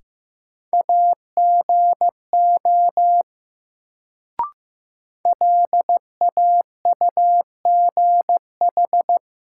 Morse code, heard as a keyed tone, about 15 wpm, 700 Hz. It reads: AGO LAUGH
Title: Sentences from Top 500 Words - New Word Drill - 15wpm